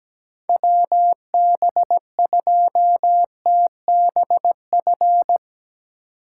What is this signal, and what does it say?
Morse code, heard as a keyed tone, about 17 wpm, 700 Hz. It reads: WB2TBF
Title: USA Call Signs - 6 Characters - 3x Character Spacing - 17wpm